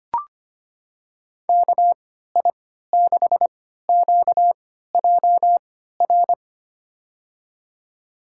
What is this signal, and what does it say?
Morse code, heard as a keyed tone, about 25 wpm, spaced as for 14 wpm, 700 Hz. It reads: KI6QJR
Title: USA Call Signs - 3x Character Spacing - 25wpm